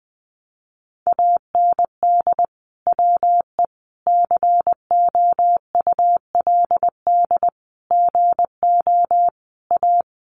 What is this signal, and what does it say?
Morse code, heard as a keyed tone, about 20 wpm, 700 Hz. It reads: AND WE COULD GO A
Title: Sentences from Top 1200 Words 20wpm